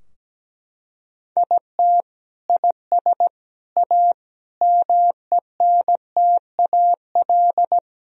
Morse code, heard as a keyed tone, about 17 wpm, 700 Hz. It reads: IT IS A MENTAL
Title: Sentences from Top 1000 Words - Review Part 4 - 17wpm